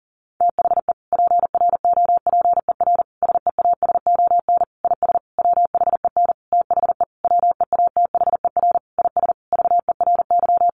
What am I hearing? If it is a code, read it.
Morse code, heard as a keyed tone, about 40 wpm, 700 Hz. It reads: THE PROPER SEASON IS WHEN THE WEATHER IS VERY
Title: Sentences from Top 1400 Words 40wpm